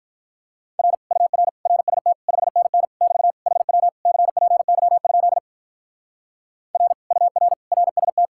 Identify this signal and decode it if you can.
Morse code, heard as a keyed tone, about 42 wpm, 700 Hz. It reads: R UR RST 5NN <BT> HW CPY? R UR RST
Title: Speed-Racing Common QSO Phrases - 35wpm